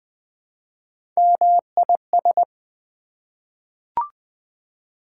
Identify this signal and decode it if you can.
Morse code, heard as a keyed tone, about 20 wpm, 700 Hz. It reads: MIS
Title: Prefixes and Suffixes 20wpm